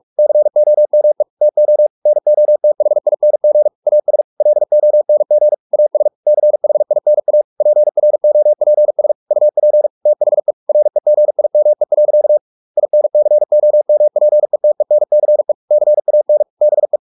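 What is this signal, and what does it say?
Morse code, heard as a keyed tone, about 44 wpm, 600 Hz. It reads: COME TO NOTHING AS LONG AS CHINA PROPS UP THE REGIME. INCOMPETENCE CAN BE
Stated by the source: Morse practice at 45 wpm, the novel Never by Ken Follett in English, from CW med Gnister.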